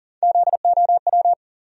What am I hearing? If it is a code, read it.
Morse code, heard as a keyed tone, about 40 wpm, 700 Hz. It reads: ZOW
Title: USA Call Sign Postfixes - 3 Letters 40wpm